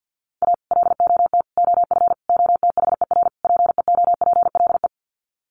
Morse code, heard as a keyed tone, about 48 wpm, 700 Hz. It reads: A LOT OF OTHER PEOPLE
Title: Sets of 5 Words Encore2 50wpm